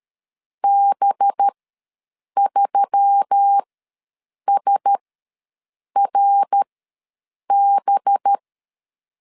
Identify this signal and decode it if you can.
Morse code, heard as a keyed tone, about 13 wpm, spaced as for 7 wpm, 800 Hz. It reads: B3SRB
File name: LEZIONI / 5 LBVC / 5 LB ws 12wpm 3034